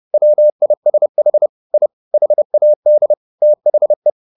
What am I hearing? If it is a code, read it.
Morse code, heard as a keyed tone, about 30 wpm, 600 Hz. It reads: WISH I HAD THE
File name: the old man and the sea / the old man and the sea 1 30wpm